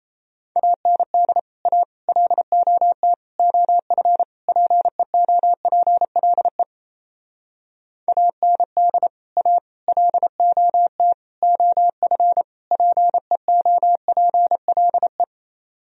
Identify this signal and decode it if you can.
Morse code, heard as a keyed tone, about 33 wpm, 700 Hz. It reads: AND A LOT OF PEOPLE AND A LOT OF PEOPLE
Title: Speed-Racing - Sets of 5 Words 28wpm